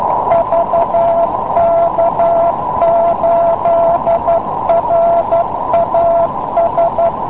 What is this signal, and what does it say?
Morse code, heard as a keyed tone, about 12 wpm, 680 Hz, with noise. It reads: VK8RAS